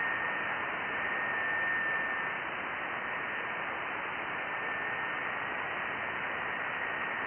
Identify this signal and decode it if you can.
FT8.
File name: IC-7100